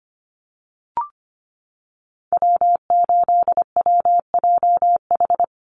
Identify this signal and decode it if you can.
Morse code, heard as a keyed tone, about 25 wpm, 690 Hz. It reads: W8WJH
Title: USA Call Signs 25wpm